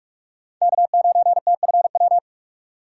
Morse code, heard as a keyed tone, about 46 wpm, 690 Hz. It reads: K9TFW
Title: USA Call Signs - 5 Characters - 6x Character Spacing - 45wpm